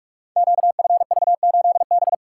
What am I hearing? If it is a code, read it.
Morse code, heard as a keyed tone, about 44 wpm, 700 Hz. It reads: QFV8B
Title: Code Groups - 5 Letter:Numbers - 45wpm